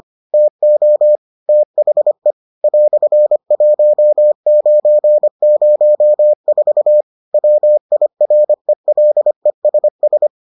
Morse code, heard as a keyed tone, about 25 wpm, 600 Hz. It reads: TO THE "1904 WIRELESS